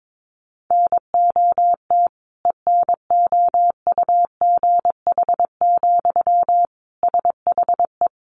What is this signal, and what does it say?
Morse code, heard as a keyed tone, about 22 wpm, 700 Hz. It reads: NOT ENOUGH, SHE
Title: Sentences from Top 1500 Words 22wpm